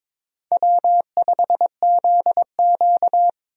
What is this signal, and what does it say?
Morse code, heard as a keyed tone, about 22 wpm, 700 Hz. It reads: W5ZQ
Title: USA Call Signs - 4 Characters - 22wpm